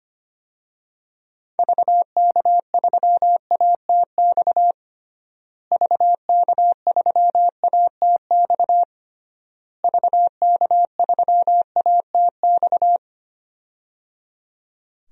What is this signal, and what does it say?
Morse code, heard as a keyed tone, about 25 wpm, 700 Hz. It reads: VK3ATX VK3ATX VK3ATX
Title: DXCC-100 Call Signs - 6 Characters - Repeated 3x - 25wpm